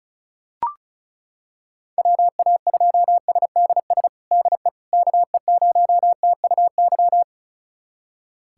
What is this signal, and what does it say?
Morse code, heard as a keyed tone, about 34 wpm, 700 Hz. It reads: WA2SDS DE KE0TUY
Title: USA Callsign DE Callsign - 6 Characters - 35wpm